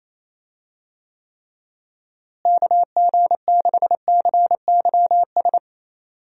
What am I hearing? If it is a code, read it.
Morse code, heard as a keyed tone, about 28 wpm, 700 Hz. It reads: KG6CYS
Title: USA Call Signs - 6 Characters - Repeated 2x - 28wpm